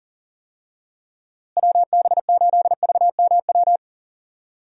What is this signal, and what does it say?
Morse code, heard as a keyed tone, about 40 wpm, 700 Hz. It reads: WB8VMW
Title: USA Call Signs - 6 Characters - Repeated 2x - 40wpm